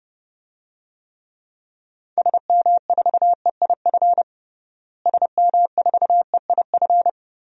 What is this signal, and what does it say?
Morse code, heard as a keyed tone, about 30 wpm, 700 Hz. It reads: SM4EIF SM4EIF
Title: DXCC-100 Call Signs - 6 Characters - Repeated 3x - 30wpm